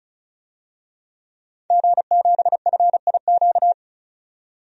Morse code, heard as a keyed tone, about 36 wpm, 700 Hz. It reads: G7FIQ
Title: DXCC-100 Call Signs - 5 Characters - 35wpm